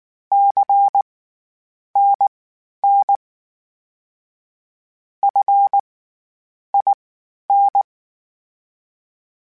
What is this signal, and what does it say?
Morse code, heard as a keyed tone, about 19 wpm, spaced as for 9 wpm, 800 Hz. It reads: CNN FIN
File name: CW Comparisons / CW Comparisons-mp3s / CW Comparisons Broken Characters @18